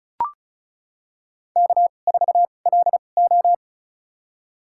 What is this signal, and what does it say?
Morse code, heard as a keyed tone, about 36 wpm, spaced as for 25 wpm, 700 Hz. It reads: K4LO
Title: USA Call Signs - 4 Characters - 2x Character Spacing - 35wpm